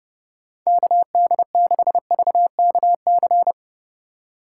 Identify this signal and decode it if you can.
Morse code, heard as a keyed tone, about 27 wpm, 700 Hz. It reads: KD6VKC